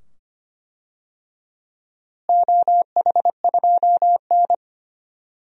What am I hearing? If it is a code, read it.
Morse code, heard as a keyed tone, about 25 wpm, 700 Hz. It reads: OH2N
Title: DXCC-100 Call Signs - 4 Characters - Repeated 2x - 25wpm